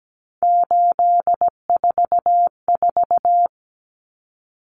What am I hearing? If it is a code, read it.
Morse code, heard as a keyed tone, about 17 wpm, 700 Hz. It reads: 844
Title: Common Cut Numbers - 3-Digit 17wpm